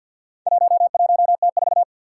Morse code, heard as a keyed tone, about 46 wpm, 700 Hz. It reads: 11T4
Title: Four Challenging Letter-Number Combinations 50wpm